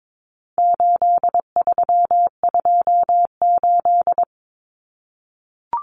Morse code, heard as a keyed tone, about 22 wpm, 700 Hz. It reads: 8328
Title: Numbers - 4-Digit 22wpm